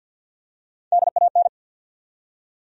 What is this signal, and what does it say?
Morse code, heard as a keyed tone, about 47 wpm, 700 Hz. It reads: DAN
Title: USA Common Names - Max 3 Letters - 50wpm